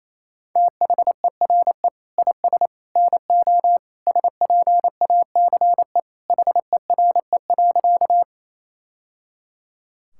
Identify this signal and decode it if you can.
Morse code, heard as a keyed tone, about 28 wpm, 700 Hz. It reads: THERE IS NO SPACE HERE.
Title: Sentences from Top 1000 Words - Review Part 5 - 28wpm